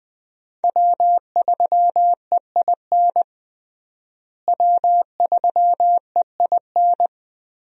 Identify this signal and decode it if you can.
Morse code, heard as a keyed tone, about 20 wpm, 700 Hz. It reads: W3EIN W3EIN
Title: USA Call Signs - 5 Characters - Repeated 3x - 20wpm